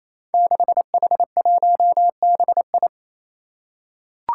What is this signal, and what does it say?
Morse code, heard as a keyed tone, about 28 wpm, 700 Hz. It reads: 6H1BI
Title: Five Challenging Letter-Number Combinations 28wpm